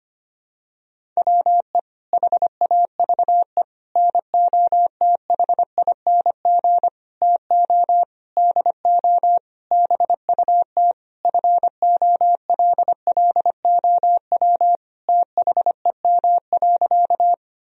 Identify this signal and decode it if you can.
Morse code, heard as a keyed tone, about 25 wpm, 700 Hz. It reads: WE HAVE NOTHING TO DO BUT FOLLOW THEM.
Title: Sentences from Top 1000 Words - Review Part 1 - 25wpm